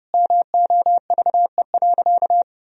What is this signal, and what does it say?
Morse code, heard as a keyed tone, about 30 wpm, 700 Hz. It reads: MOVE.
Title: Sentences from Top 1000 Words - Review Part 1 - 30wpm